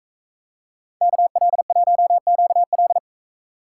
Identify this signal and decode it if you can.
Morse code, heard as a keyed tone, about 42 wpm, 700 Hz. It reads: KL1QL